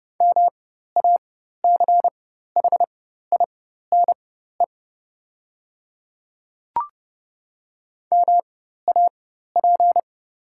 Morse code, heard as a keyed tone, about 31 wpm, spaced as for 14 wpm, 700 Hz. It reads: MACHINE MAP
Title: 4x Character Spacing - Top 500 Words - 30wpm